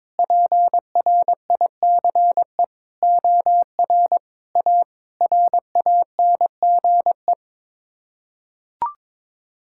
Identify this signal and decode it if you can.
Morse code, heard as a keyed tone, about 22 wpm, 700 Hz. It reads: PRICE OR A RANGE
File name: Sets of 9 Words - 22wpm